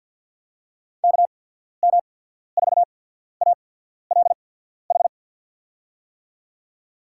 Morse code, heard as a keyed tone, about 52 wpm, spaced as for 14 wpm, 700 Hz. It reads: KM4ALH
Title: USA Call Signs - 6 Characters - 8x Character Spacing - 50wpm